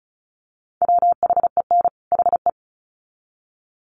Morse code, heard as a keyed tone, about 34 wpm, 700 Hz. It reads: WHEN HE
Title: Sets of 2 Words 35wpm